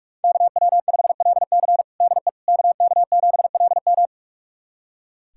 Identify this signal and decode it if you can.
Morse code, heard as a keyed tone, about 44 wpm, 700 Hz. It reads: KW5LC DE KK7LM
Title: USA Callsign DE Callsign - 5 Characters - 45wpm